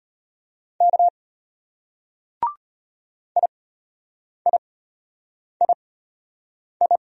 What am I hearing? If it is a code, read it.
Morse code, heard as a keyed tone, about 32 wpm, spaced as for 8 wpm, 700 Hz. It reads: K IIII